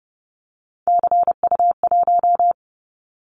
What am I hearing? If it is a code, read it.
Morse code, heard as a keyed tone, about 30 wpm, 700 Hz. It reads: CU1
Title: Callsign Prefixes for Top 50 DXCC - 30wpm